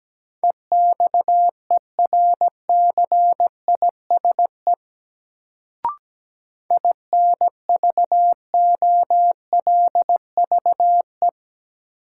Morse code, heard as a keyed tone, about 17 wpm, 700 Hz. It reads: EXERCISE INVOLVE
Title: Sentences from Top 1200 Words - New Word Drill Alt - 17wpm